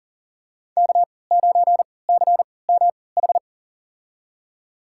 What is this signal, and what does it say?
Morse code, heard as a keyed tone, about 40 wpm, spaced as for 23 wpm, 700 Hz. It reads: K9CMH
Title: USA Call Signs - 5 Characters - 3x Character Spacing - 40wpm